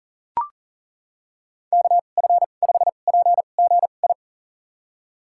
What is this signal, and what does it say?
Morse code, heard as a keyed tone, about 37 wpm, spaced as for 28 wpm, 700 Hz. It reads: KF5PGI
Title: USA Call Signs - 6 Characters - 2x Character Spacing - 40wpm